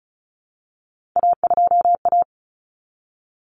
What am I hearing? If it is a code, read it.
Morse code, heard as a keyed tone, about 34 wpm, 700 Hz. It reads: A2A
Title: Common Cut Numbers - 3-Digit 35wpm